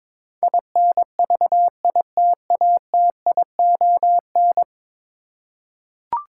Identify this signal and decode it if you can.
Morse code, heard as a keyed tone, about 22 wpm, 700 Hz. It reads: INVITATION
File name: practice / 22wpm / 10 Letter Words - First 3 Letters Given - 22wpm